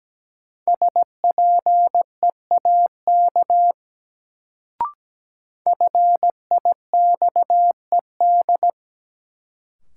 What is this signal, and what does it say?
Morse code, heard as a keyed tone, about 17 wpm, 700 Hz. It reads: SPEAK FIXED
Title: Sentences from Top 1100 Words - New Word Drill Alt - 17wpm